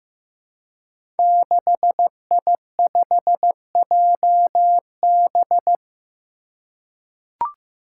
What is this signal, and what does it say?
Morse code, heard as a keyed tone, about 16 wpm, 700 Hz. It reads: 6I5JB
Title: Five Challenging Letter-Number Combinations 15wpm